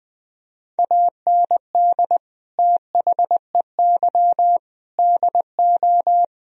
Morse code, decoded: AND THEY DO

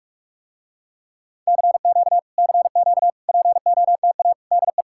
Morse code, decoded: CQ CQ POTA DE

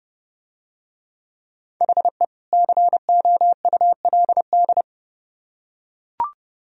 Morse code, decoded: HE COULD

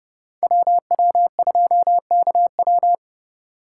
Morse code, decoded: WW2KW

